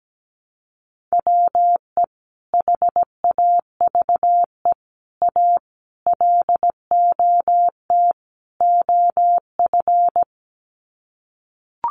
Morse code, heard as a keyed tone, about 17 wpm, 700 Hz. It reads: WE HAVE A LOT OF
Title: Sets of 5 Words 17wpm